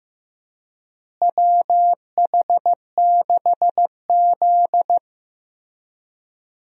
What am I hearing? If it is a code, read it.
Morse code, heard as a keyed tone, about 15 wpm, 700 Hz. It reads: WH6Z
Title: USA Call Signs - 4 Characters - Repeated 2x - 15wpm